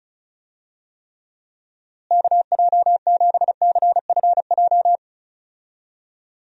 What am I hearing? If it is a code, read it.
Morse code, heard as a keyed tone, about 34 wpm, 700 Hz. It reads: KJ7CFJ
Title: USA Call Signs - 6 Characters - Repeated 2x - 35wpm